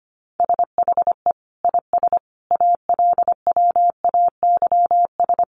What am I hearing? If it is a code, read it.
Morse code, heard as a keyed tone, about 25 wpm, 700 Hz. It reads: SHE IS ALWAYS
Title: Sentences from Top 1200 Words 25wpm